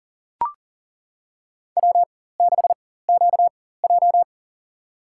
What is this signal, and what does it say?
Morse code, heard as a keyed tone, about 40 wpm, spaced as for 19 wpm, 700 Hz. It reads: W6QJ
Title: USA Call Signs - 4 Characters - 4x Character Spacing - 40wpm